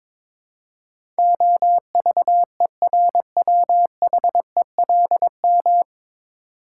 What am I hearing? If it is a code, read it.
Morse code, heard as a keyed tone, about 22 wpm, 700 Hz. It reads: OVERWHELM
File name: 9 Letter Words - First 3 Letters Given - 22wpm